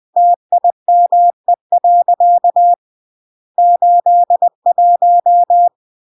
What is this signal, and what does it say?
Morse code, heard as a keyed tone, about 20 wpm, 700 Hz. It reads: TIME. 81
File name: CW / jodorowsky / 82 commandments 20-20